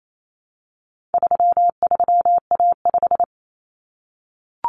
Morse code, heard as a keyed tone, about 28 wpm, 700 Hz. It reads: 33A5
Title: Common Cut Numbers - 4-Digit 28wpm